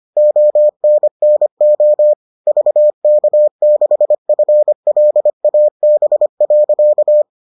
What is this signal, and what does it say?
Morse code, heard as a keyed tone, about 25 wpm, 600 Hz. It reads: ONNO VK6FLAB.